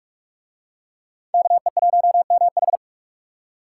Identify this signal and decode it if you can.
Morse code, heard as a keyed tone, about 42 wpm, 700 Hz. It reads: KE1MH